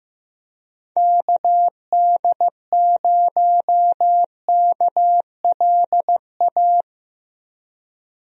Morse code, heard as a keyed tone, about 15 wpm, 700 Hz. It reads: KD0KLA
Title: USA Call Signs - 6 Characters - Repeated 3x - 15wpm